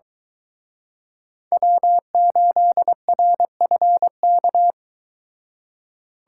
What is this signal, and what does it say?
Morse code, heard as a keyed tone, about 23 wpm, 700 Hz. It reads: W8RFK